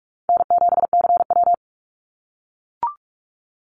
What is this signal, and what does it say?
Morse code, heard as a keyed tone, about 44 wpm, 700 Hz. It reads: N7CW
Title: NA SOTA Activator Callsigns 45wpm